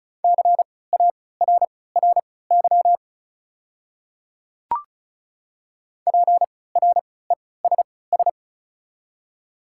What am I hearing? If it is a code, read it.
Morse code, heard as a keyed tone, about 34 wpm, spaced as for 20 wpm, 700 Hz. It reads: CARRY PRESS
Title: 3x Character Spacing - Top 300 Words - 35wpm